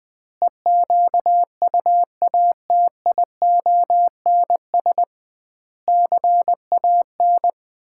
Morse code, heard as a keyed tone, about 20 wpm, 700 Hz. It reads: EQUATIONS CAN
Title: 2x Word-Spacing - Sets of 5 Words - 20wpm